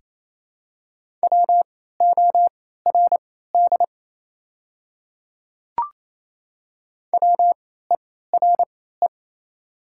Morse code, heard as a keyed tone, about 28 wpm, spaced as for 16 wpm, 700 Hz. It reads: WORD WERE
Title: 3x Character Spacing - Top 100 Words - 28wpm